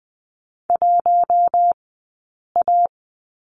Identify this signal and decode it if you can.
Morse code, heard as a keyed tone, about 20 wpm, 700 Hz. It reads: 1 A
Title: 092 Lesson 16d - Letter c - Two Characters 20wpm